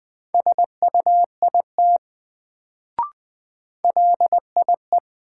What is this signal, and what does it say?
Morse code, heard as a keyed tone, about 20 wpm, 700 Hz. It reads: SUIT LIE
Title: Sentences from Top 1600 Words - New Word Drill - 20wpm